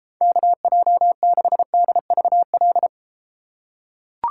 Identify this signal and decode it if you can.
Morse code, heard as a keyed tone, about 32 wpm, 700 Hz. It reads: KJ6DVL